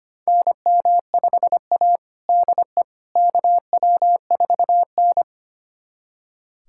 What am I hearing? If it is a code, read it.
Morse code, heard as a keyed tone, about 25 wpm, 700 Hz. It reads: NM5A DE KW4N
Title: USA Callsign DE Callsign - 4 Characters - 25wpm